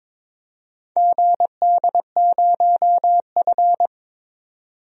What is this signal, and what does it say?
Morse code, heard as a keyed tone, about 21 wpm, 700 Hz. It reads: GD0F
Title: DXCC-100 Call Signs - 4 Characters - 22wpm